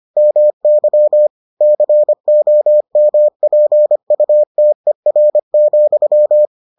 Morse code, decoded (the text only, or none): MY COMPUTER,